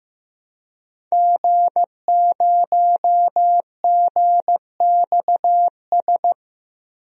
Morse code, heard as a keyed tone, about 15 wpm, 700 Hz. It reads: G0GXS
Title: DXCC-100 Call Signs - 5 Characters - 15wpm